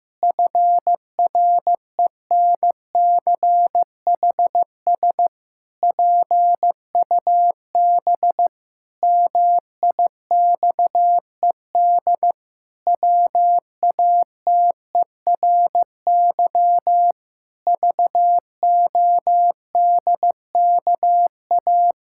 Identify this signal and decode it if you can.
Morse code, heard as a keyed tone, about 15 wpm, 700 Hz. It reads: FRENCHS PUB MIXED WATERY VODKA
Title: Pangram Challenge v2 - 15wpm